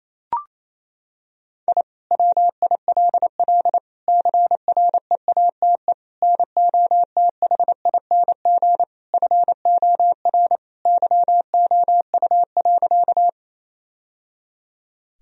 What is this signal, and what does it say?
Morse code, heard as a keyed tone, about 28 wpm, 700 Hz. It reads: I WILL CREATE NOTHING FOR YOU.
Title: Sentences from Top 1000 Words - Review Part 3 - 28wpm